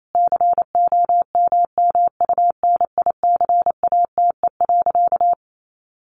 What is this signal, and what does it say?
Morse code, heard as a keyed tone, about 28 wpm, 700 Hz. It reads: COMMUNICATE.
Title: Sentences from Top 1300 Words 28wpm